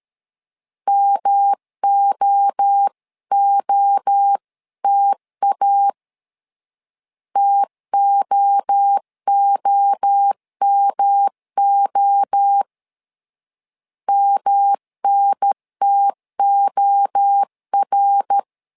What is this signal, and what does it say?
Morse code, heard as a keyed tone, about 13 wpm, 800 Hz. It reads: MOOTA TOOMO MNTOR
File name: LEZIONI / 3 MWRO / 3 MWRO 12wpm 1730